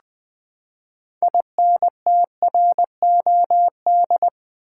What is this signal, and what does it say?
Morse code, heard as a keyed tone, about 20 wpm, 700 Hz. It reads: INTROD
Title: Build A Word - 6 Letters At A Time - 12-Letter Words - 20wpm